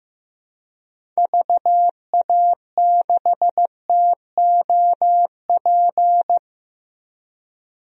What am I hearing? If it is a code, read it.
Morse code, heard as a keyed tone, about 15 wpm, 700 Hz. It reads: VA6TOP
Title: DXCC-100 Call Signs - 6 Characters - 15wpm